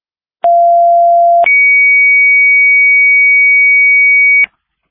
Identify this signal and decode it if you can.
Two-tone page, heard as a one-second tone, then a three-second tone, about 690 then 2250 Hz, with noise. It is heard from 0.4 to 4.5 s.